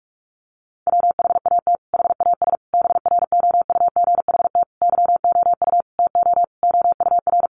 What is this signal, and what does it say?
Morse code, heard as a keyed tone, about 44 wpm, 700 Hz. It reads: WHAT HAS BROUGHT YOU TO OUR
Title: Sentences from Top 1500 Words 45wpm